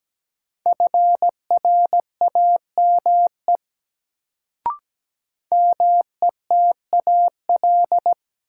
Morse code, heard as a keyed tone, about 17 wpm, 700 Hz. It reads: FRAME METAL
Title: Sentences from Top 1000 Words - Review Part 6 - New Word Drill - 17wpm